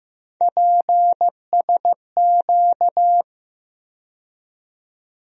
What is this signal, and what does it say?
Morse code, heard as a keyed tone, about 15 wpm, 700 Hz. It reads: PSQ
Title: USA Call Sign Postfixes - 3 Letters 15wpm